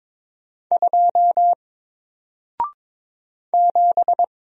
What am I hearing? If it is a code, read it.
Morse code, heard as a keyed tone, about 22 wpm, 700 Hz. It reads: 2 7
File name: Single-Letter-Number-22wpm